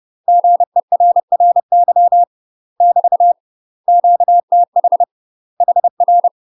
Morse code, heard as a keyed tone, about 30 wpm, 700 Hz. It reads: GERRY <BT> QTH HR
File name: UK MASTER QSOs TRAINING / 30WPM / UK MASTER QSOs 0034